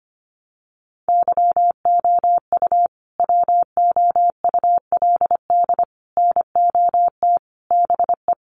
Morse code, heard as a keyed tone, about 25 wpm, 700 Hz. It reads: YOU WOULD NOT BE
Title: Sets of 5 Words Encore2 25wpm